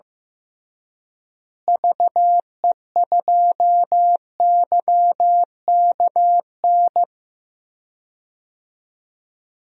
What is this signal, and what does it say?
Morse code, heard as a keyed tone, about 15 wpm, 700 Hz. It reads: VE2YKN
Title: DXCC-100 Call Signs - 6 Characters - Repeated 3x - 15wpm